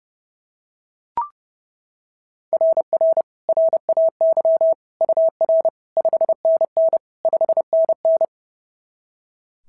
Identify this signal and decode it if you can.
Morse code, heard as a keyed tone, about 30 wpm, 690 Hz. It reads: RR RAY UR 5NN 5NN
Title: SOTA QSO Practice 30wpm